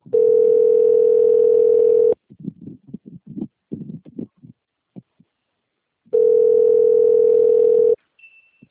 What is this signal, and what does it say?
Ringback tone (North America), 440 + 480 Hz.
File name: out- - - - - .5209